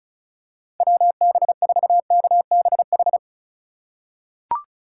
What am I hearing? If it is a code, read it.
Morse code, heard as a keyed tone, about 36 wpm, 700 Hz. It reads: WB4KBH
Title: USA Call Signs - 8x Character Spacing - 35wpm